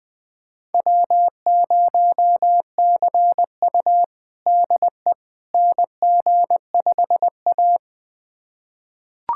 Morse code, heard as a keyed tone, about 21 wpm, 700 Hz. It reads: W0CU DE NG5A
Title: USA Callsign DE Callsign - 4 Characters - 20wpm